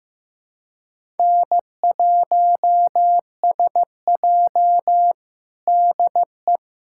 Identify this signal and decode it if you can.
Morse code, heard as a keyed tone, about 15 wpm, 700 Hz. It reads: N1SJ DE
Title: USA Callsign DE Callsign - 4 Characters - 15wpm